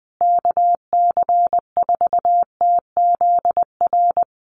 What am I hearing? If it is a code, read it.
Morse code, heard as a keyed tone, about 20 wpm, 700 Hz. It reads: KC4TZR
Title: USA Call Signs Repeated 3x 20wpm